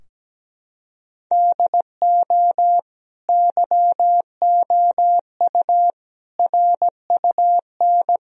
Morse code, heard as a keyed tone, about 17 wpm, 700 Hz. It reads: DO YOU RUN
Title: Sentences from Top 1000 Words - Review Part 4 - 17wpm